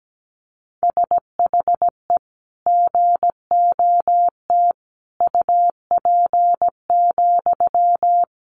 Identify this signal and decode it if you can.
Morse code, heard as a keyed tone, about 17 wpm, 700 Hz. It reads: SHE GOT UP,
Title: Sentences from Top 500 Words 17wpm